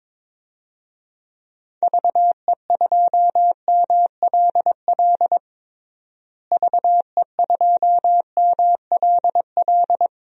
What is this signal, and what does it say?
Morse code, heard as a keyed tone, about 21 wpm, 700 Hz. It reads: VE2MLL VE2MLL
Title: DXCC-100 Call Signs - 6 Characters - Repeated 2x - 22wpm